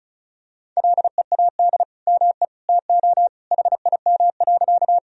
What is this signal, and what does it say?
Morse code, heard as a keyed tone, about 36 wpm, 700 Hz. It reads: LEAD ME TO HIM.
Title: Sentences from Top 1000 Words - Review Part 2 - 35wpm